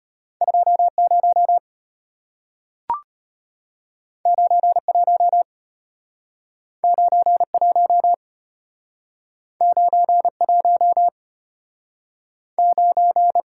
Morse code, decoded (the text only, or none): 20 91 91 91 9